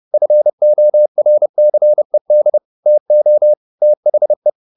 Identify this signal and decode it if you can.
Morse code, heard as a keyed tone, about 30 wpm, 600 Hz. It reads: FORCED TO THE
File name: the old man and the sea / the old man and the sea 1 30wpm